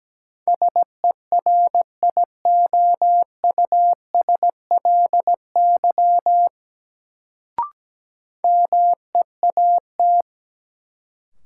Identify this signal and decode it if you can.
Morse code, heard as a keyed tone, about 17 wpm, 700 Hz. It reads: SERIOUSLY MEAT
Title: Sentences from Top 1200 Words - New Word Drill Alt - 17wpm